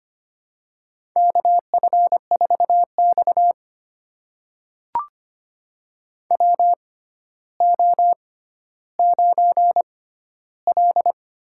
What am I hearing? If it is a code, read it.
Morse code, heard as a keyed tone, about 25 wpm, spaced as for 13 wpm, 700 Hz. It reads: KF4X WO9L
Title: USA Call Signs - 4 Characters - 6x Character Spacing - 25wpm